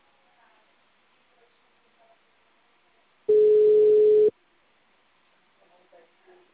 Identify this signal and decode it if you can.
Ringback tone (Europe, 425 Hz), with speech.